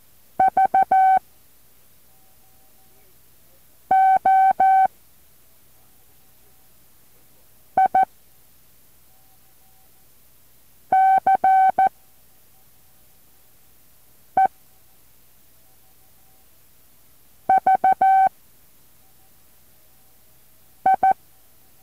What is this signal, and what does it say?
Morse code, heard as a keyed tone, about 14 wpm, 760 Hz. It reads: V O I C E V I
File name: intl morse code2